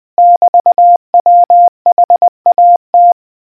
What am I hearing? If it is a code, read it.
Morse code, heard as a keyed tone, about 20 wpm, 700 Hz. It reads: <BT> WHAT